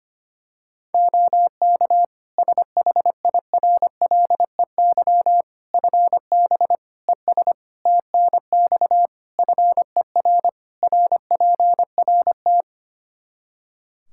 Morse code, heard as a keyed tone, about 25 wpm, 700 Hz. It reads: OK SHIRLEY FB ES TNX FER RPRT
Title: Practice QSO - Exchange 2 - Info - 25wpm